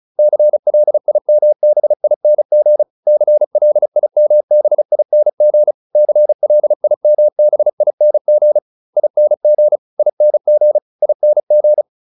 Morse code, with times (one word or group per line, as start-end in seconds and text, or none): 0.2-2.8 CLIMBING
3.1-5.7 CLIMBING
6.0-8.6 CLIMBING
9.0-9.8 ING
10.0-10.8 ING
11.0-11.8 ING